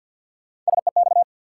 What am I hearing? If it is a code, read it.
Morse code, heard as a keyed tone, about 52 wpm, 700 Hz. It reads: SEX
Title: Sentences from Top 1600 Words - New Word Drill Alt - 50wpm